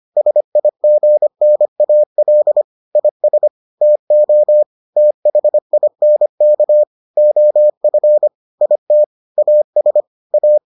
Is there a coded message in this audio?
Yes: Morse code from 0.2 to 10.6 s, heard as a keyed tone, about 25 wpm, 600 Hz. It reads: SIGNAL IS TO THINK OF IT AS A